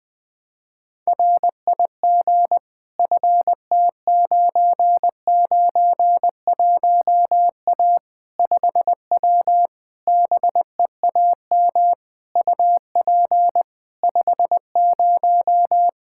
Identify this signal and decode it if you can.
Morse code, heard as a keyed tone, about 20 wpm, 700 Hz. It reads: RIG FT991A 5W BEAM UP 50
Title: Practice QSO - Exchange 2 - Info - 20wpm